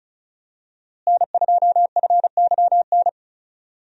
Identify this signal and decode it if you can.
Morse code, heard as a keyed tone, about 34 wpm, 700 Hz. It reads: N2FYN
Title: USA Call Signs - 5 Characters - Repeated 2x - 35wpm